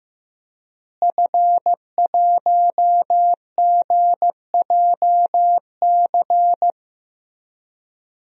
Morse code, heard as a keyed tone, about 15 wpm, 700 Hz. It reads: F1GJC
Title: DXCC-100 Call Signs - 5 Characters - Repeated 3x - 15wpm